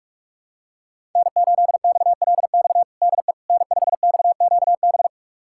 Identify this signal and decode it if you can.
Morse code, heard as a keyed tone, about 41 wpm, 700 Hz. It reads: N8XLX DE N5XQB
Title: USA Callsign DE Callsign - 5 Characters - 45wpm